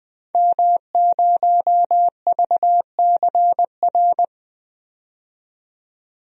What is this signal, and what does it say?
Morse code, heard as a keyed tone, about 20 wpm, 700 Hz. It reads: M0VCR
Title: DXCC-100 Call Signs - 5 Characters - 20wpm